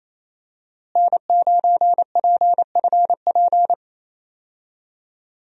Morse code, heard as a keyed tone, about 28 wpm, 700 Hz. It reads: N9PFP
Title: USA Call Signs - 5 Characters - Repeated 2x - 28wpm